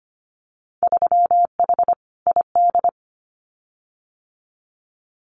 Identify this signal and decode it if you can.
Morse code, heard as a keyed tone, about 25 wpm, 700 Hz. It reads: 3H ID
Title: Winter Field Day Exchanges Encore 25wpm